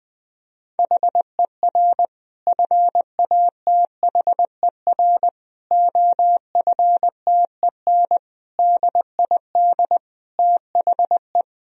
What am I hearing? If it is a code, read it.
Morse code, heard as a keyed tone, about 20 wpm, 700 Hz. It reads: HER FATHER OFTEN DID THE
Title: Sentences from Top 1000 Words - Review Part 1 - 20wpm